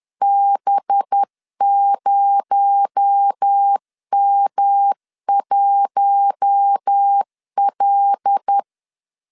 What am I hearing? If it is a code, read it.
Morse code, heard as a keyed tone, about 10 wpm, 800 Hz. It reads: B0M1L
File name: LEZIONI / 7 GF9Z / 7 G 10wpm 2501